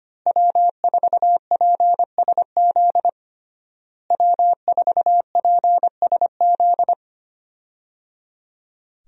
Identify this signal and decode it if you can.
Morse code, heard as a keyed tone, about 25 wpm, 700 Hz. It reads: W4PSZ W4PSZ
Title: USA Call Signs - 5 Characters - Repeated 3x - 25wpm